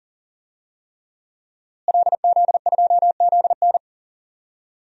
Morse code, heard as a keyed tone, about 40 wpm, 700 Hz. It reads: LZ2ZN
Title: DXCC-100 Call Signs - 5 Characters - Repeated 3x - 40wpm